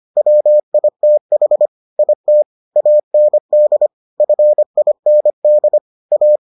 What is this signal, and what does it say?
Morse code, heard as a keyed tone, about 25 wpm, 600 Hz. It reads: WITH IT AND FIND A